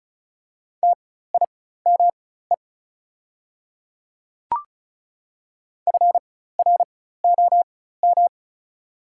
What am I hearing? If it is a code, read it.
Morse code, heard as a keyed tone, about 34 wpm, spaced as for 16 wpm, 690 Hz. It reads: TIME FROM
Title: 4x Character Spacing - Top 100 Words - 35wpm